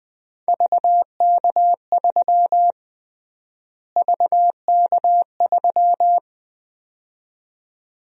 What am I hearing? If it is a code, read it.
Morse code, heard as a keyed tone, about 20 wpm, 700 Hz. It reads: VK3 VK3
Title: USA Call Sign Prefixes - Repeated 2x - 20wpm